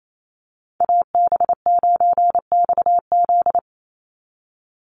Morse code, heard as a keyed tone, about 28 wpm, 700 Hz. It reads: AB9XZ
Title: USA Call Signs 28wpm